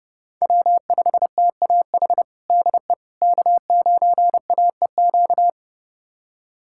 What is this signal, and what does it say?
Morse code, heard as a keyed tone, about 30 wpm, 700 Hz. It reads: W5TAH DE K9AEQ
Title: USA Callsign DE Callsign - 5 Characters - 30wpm